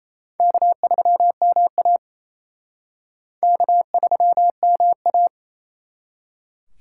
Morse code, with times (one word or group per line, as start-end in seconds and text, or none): 0.4-2.0 K3MA
3.4-5.3 K3MA